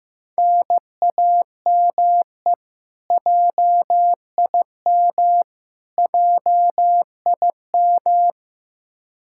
Morse code, heard as a keyed tone, about 16 wpm, 700 Hz. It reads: NAME JIM JIM
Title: Practice QSO - Exchange 1 - Beginning - 15wpm